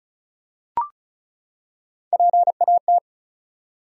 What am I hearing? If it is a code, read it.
Morse code, heard as a keyed tone, about 35 wpm, 700 Hz. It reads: PAT
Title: USA Common Names - Max 5 Letters - 35wpm